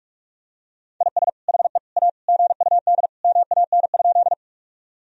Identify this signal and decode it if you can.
Morse code, heard as a keyed tone, about 46 wpm, 700 Hz. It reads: IS HE A GUD MAN?
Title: Sentences from Top 200 Words and Abbreviations 45wpm